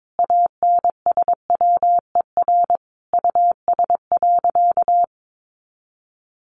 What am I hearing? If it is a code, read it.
Morse code, heard as a keyed tone, about 22 wpm, 700 Hz. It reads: ANSWER US.
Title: Sentences from Top 1500 Words 22wpm